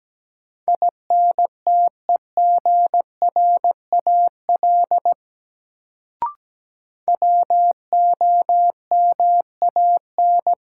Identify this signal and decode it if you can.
Morse code, heard as a keyed tone, about 17 wpm, 700 Hz. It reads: INTEGRAL WOMAN
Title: Sets of 5 Words - New Word Drill - 17wpm